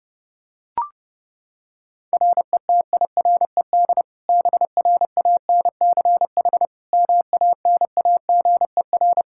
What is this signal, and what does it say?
Morse code, heard as a keyed tone, about 30 wpm, 700 Hz. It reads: RETIRED BRANCH MANAGER
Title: Practice QSO - Exchange 3 - More Info - 30wpm